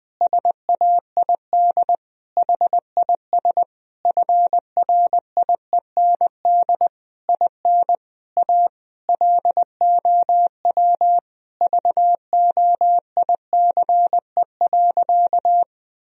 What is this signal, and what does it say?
Morse code, heard as a keyed tone, about 20 wpm, 700 Hz. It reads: SAID HIS FRIEND IN A LOW VOICE.
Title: Sentences from Top 1000 Words - Review Part 1 - 20wpm